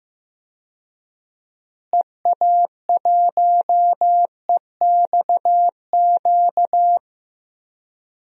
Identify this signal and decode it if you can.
Morse code, heard as a keyed tone, about 15 wpm, 700 Hz. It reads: EA1EXQ
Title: DXCC-100 Call Signs - 6 Characters - Repeated 2x - 15wpm